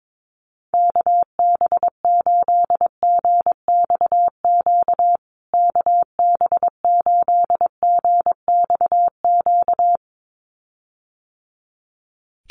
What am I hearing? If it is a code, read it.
Morse code, heard as a keyed tone, about 22 wpm, 700 Hz. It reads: KB8GXQ KB8GXQ
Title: USA Call Signs Repeated 2x 22wpm